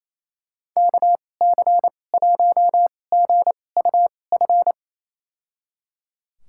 Morse code, heard as a keyed tone, about 28 wpm, spaced as for 20 wpm, 700 Hz. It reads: KC1GUF
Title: USA Call Signs - 2x Character Spacing - 28wpm